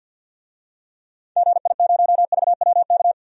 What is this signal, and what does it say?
Morse code, heard as a keyed tone, about 52 wpm, 700 Hz. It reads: GI0VWK
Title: DXCC-100 Call Signs - 6 Characters - 50wpm